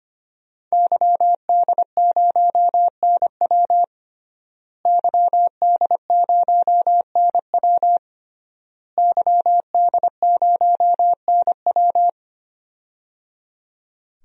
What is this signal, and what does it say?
Morse code, heard as a keyed tone, about 25 wpm, 700 Hz. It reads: YD0NW YD0NW YD0NW
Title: DXCC-100 Call Signs - 5 Characters - Repeated 3x - 25wpm